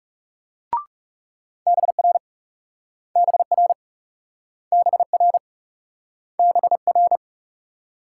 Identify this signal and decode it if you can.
Morse code, heard as a keyed tone, about 38 wpm, 700 Hz. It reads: BR BR BR BR